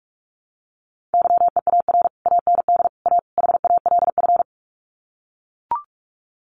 Morse code, heard as a keyed tone, about 44 wpm, 700 Hz. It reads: YEAR AND A HALF
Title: Sets of 4 Words 45wpm